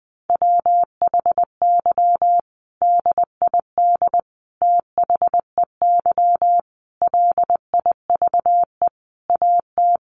Morse code, heard as a keyed tone, about 20 wpm, 700 Hz. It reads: WHY DID THEY LIVE AT
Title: Sentences from Top 900 Words 20wpm